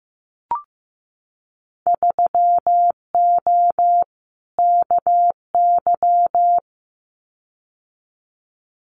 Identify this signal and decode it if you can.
Morse code, heard as a keyed tone, about 15 wpm, 700 Hz. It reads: 3O KY
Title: Winter Field Day Exchanges 15wpm